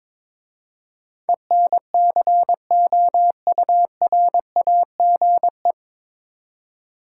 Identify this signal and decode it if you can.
Morse code, heard as a keyed tone, about 22 wpm, 700 Hz. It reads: ENCOURAGE
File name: 9 Letter Words - First 3 Letters Given - 22wpm